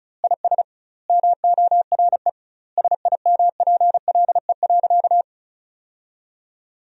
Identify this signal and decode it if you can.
Morse code, heard as a keyed tone, about 36 wpm, 700 Hz. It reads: IS MORE SIMPLE.
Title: 2x Word-Spacing - Sentences from Top 500 Words - 35wpm